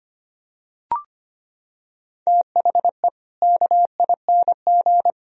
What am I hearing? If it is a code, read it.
Morse code, heard as a keyed tone, about 25 wpm, 700 Hz. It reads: THE KING
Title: Sentences from Top 1000 Words - Review Part 3 - 25wpm